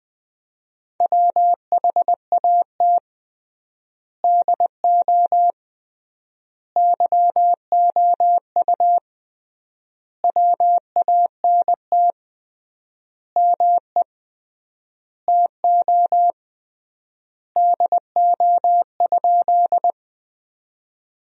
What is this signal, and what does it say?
Morse code, heard as a keyed tone, about 20 wpm, 700 Hz. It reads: WHAT DO YOU WANT ME TO DO?
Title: 3x Word-Spacing - Sentences from Top 200 Words - 20wpm